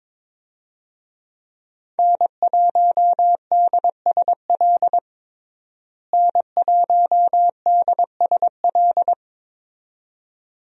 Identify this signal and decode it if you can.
Morse code, heard as a keyed tone, about 21 wpm, 700 Hz. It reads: N1DSL N1DSL
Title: USA Call Signs - 5 Characters - Repeated 2x - 22wpm